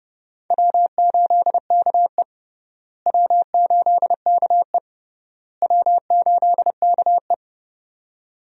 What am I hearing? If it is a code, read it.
Morse code, heard as a keyed tone, about 30 wpm, 700 Hz. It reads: W8KE W8KE W8KE
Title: USA Call Signs - 4 Characters - Repeated 3x - 30wpm